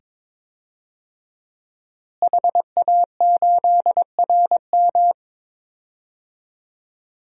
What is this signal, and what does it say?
Morse code, heard as a keyed tone, about 22 wpm, 700 Hz. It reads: HA8RM
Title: DXCC-100 Call Signs - 5 Characters - Repeated 2x - 22wpm